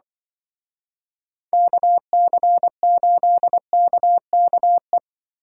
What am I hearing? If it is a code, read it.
Morse code, heard as a keyed tone, about 25 wpm, 700 Hz. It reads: KC8KKE